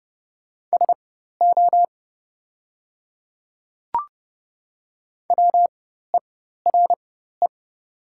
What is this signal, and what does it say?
Morse code, heard as a keyed tone, about 30 wpm, spaced as for 14 wpm, 700 Hz. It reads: SO WERE
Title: 4x Character Spacing - Top 100 Words - 30wpm